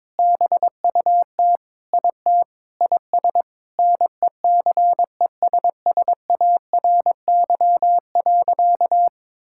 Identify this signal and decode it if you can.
Morse code, heard as a keyed tone, about 22 wpm, 700 Hz. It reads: BUT IT IS NECESSARY.
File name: Sentences with 9-Word Phrases - 22wpm